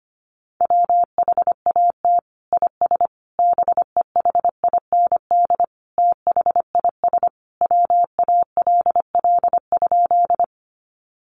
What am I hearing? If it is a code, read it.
Morse code, heard as a keyed tone, about 25 wpm, 700 Hz. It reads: WHAT IS BEHIND THIS WALL?
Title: Sentences from Top 1100 Words 25wpm